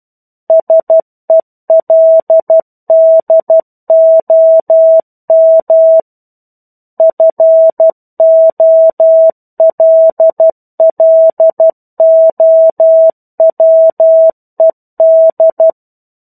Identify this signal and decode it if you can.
Morse code, heard as a keyed tone, about 12 wpm, 650 Hz. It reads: SELDOM FOLLOWED
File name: 01-Chapter-01 12 WPM